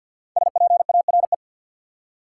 Morse code, heard as a keyed tone, about 50 wpm, 700 Hz. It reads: SPARE
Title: Five Letter Words - 50wpm